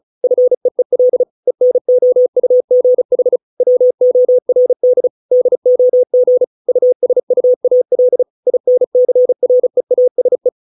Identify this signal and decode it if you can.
Morse code, heard as a keyed tone, about 34 wpm, 500 Hz. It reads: FEEL ENOUGH WORD DOG USUAL INCREASE